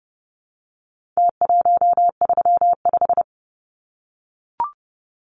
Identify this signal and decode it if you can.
Morse code, heard as a keyed tone, about 30 wpm, 700 Hz. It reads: T135
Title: Numbers - Contest Style - 4-Digit 30wpm